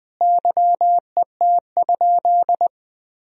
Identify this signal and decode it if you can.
Morse code, heard as a keyed tone, about 19 wpm, 700 Hz. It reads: YET?
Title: Sentences from Top 1000 Words - Review Part 2 - 20wpm